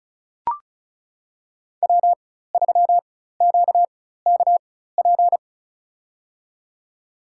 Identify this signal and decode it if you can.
Morse code, heard as a keyed tone, about 36 wpm, spaced as for 16 wpm, 700 Hz. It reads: W3QKP
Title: USA Call Signs - 4x Character Spacing - 35wpm